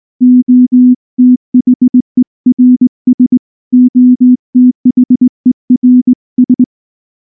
Morse code, heard as a keyed tone, about 18 wpm, 250 Hz. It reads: OTHERS OTHERS
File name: morse / morse-training-20WPM-3